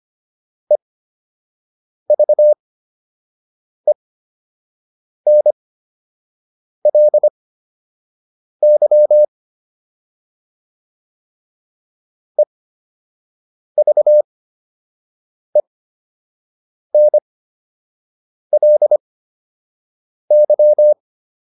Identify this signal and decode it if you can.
Morse code, heard as a keyed tone, about 25 wpm, spaced as for 6 wpm, 600 Hz. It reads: EVENLY EVENLY